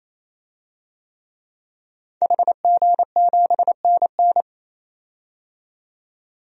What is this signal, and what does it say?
Morse code, heard as a keyed tone, about 28 wpm, 700 Hz. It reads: HG7NN
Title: DXCC-100 Call Signs - 5 Characters - 28wpm